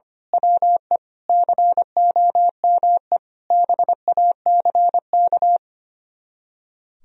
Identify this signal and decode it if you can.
Morse code, heard as a keyed tone, about 25 wpm, 700 Hz. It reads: WE COME BACK
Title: Speed-Racing - Sets of 5 Words 25wpm